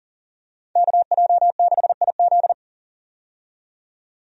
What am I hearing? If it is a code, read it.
Morse code, heard as a keyed tone, about 40 wpm, 700 Hz. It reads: KJ6IZ
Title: USA Call Signs - 5 Characters - 40wpm